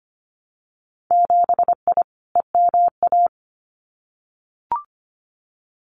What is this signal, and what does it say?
Morse code, heard as a keyed tone, about 25 wpm, 700 Hz. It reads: 7I EMA
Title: Winter Field Day Exchanges Encore 25wpm